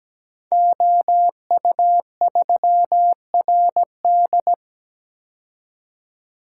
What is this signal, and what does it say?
Morse code, heard as a keyed tone, about 17 wpm, 700 Hz. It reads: OU3RD
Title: Code Groups - 5 Letter:Numbers - 17wpm